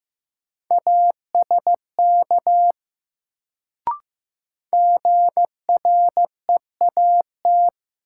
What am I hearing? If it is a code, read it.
Morse code, heard as a keyed tone, about 15 wpm, 700 Hz. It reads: ASK GREAT
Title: Sentences from Top 200 Words - New Word Drill - 15wpm